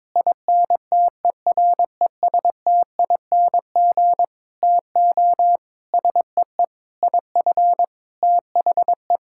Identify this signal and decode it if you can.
Morse code, heard as a keyed tone, about 21 wpm, 700 Hz. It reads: INTERESTING TO SEE IF THE
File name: practice / 22wpm / Sentences with 8-Word Phrases - 22wpm